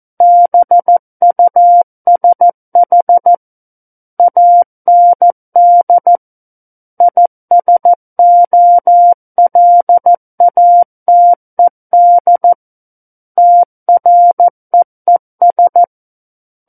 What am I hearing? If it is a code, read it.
Morse code, heard as a keyed tone, about 14 wpm, 700 Hz. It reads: BUSH AND ISOLATED TREES
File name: Book One / 05-Chapter 5 - 14 WPM